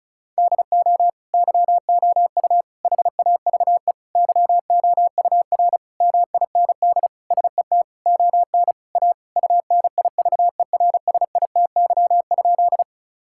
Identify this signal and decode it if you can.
Morse code, heard as a keyed tone, about 34 wpm, 700 Hz. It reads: DO YOU HAVE YOUR MIND SET ON A UNIVERSITY?
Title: Sentences from Top 1000 Words - Review Part 5 - 35wpm